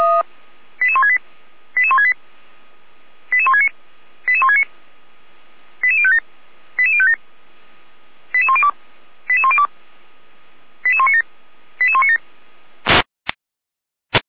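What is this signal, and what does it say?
Five-tone selective call (ZVEI), heard as quick runs of five tones, tones of about 70 ms. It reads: 88168 88168 88169 88169 88057 88057 88102 88102 88197 88197
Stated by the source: ZVEI five-tone selective call 88367 received by SDR.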